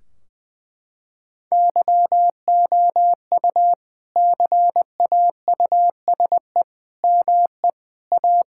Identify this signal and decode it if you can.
Morse code, heard as a keyed tone, about 21 wpm, 700 Hz. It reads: YOU CAUSE ME A
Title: Sentences from Top 1600 Words 20wpm